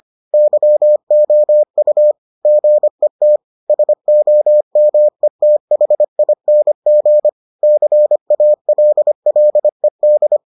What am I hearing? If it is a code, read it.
Morse code, heard as a keyed tone, about 25 wpm, 600 Hz. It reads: YOU GET SOMETHING CALLED